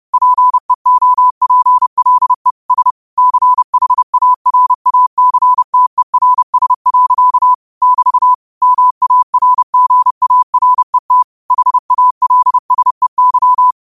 Morse code, heard as a keyed tone, about 30 wpm, 1000 Hz. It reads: PEOPLE S CHARACTERS. <BT> MARGARET HALSEY
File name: qotdfiles / 30wpm